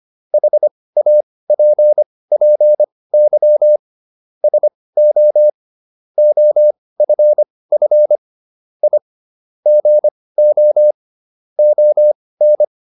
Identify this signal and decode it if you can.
Morse code, heard as a keyed tone, about 25 wpm, spaced as for 18 wpm, 600 Hz. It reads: HAPPY SO OFF I GO ON